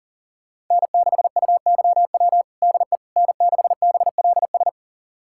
Morse code, heard as a keyed tone, about 40 wpm, 700 Hz. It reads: N6UYW DE N6BLS
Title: USA Callsign DE Callsign - 5 Characters - 40wpm